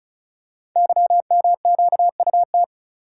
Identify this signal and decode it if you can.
Morse code, heard as a keyed tone, about 35 wpm, 700 Hz. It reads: YMQUT
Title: Code Groups - 5 Letters - 35wpm